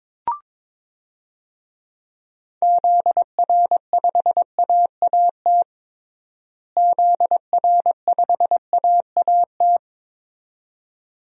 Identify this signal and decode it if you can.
Morse code, heard as a keyed tone, about 22 wpm, 700 Hz. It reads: ZR5AAT ZR5AAT
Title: DXCC-100 Call Signs - 6 Characters - Repeated 2x - 22wpm